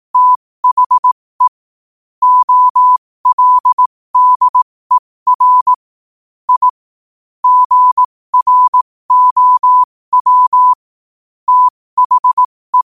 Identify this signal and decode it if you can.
Morse code, heard as a keyed tone, about 18 wpm, spaced as for 15 wpm, 1000 Hz. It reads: THE OLDER I GROW THE
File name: qotdfiles / 15wpm